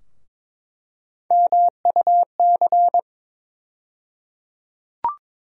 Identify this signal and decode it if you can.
Morse code, heard as a keyed tone, about 22 wpm, 700 Hz. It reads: MUC
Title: USA Call Sign Postfixes - 3 Letters 22wpm